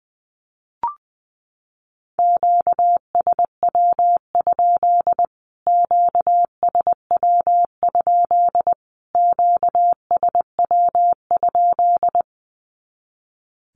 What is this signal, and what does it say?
Morse code, heard as a keyed tone, about 19 wpm, 700 Hz. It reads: QSW? QSW? QSW?
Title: Q-Codes Repeated 3x 20wpm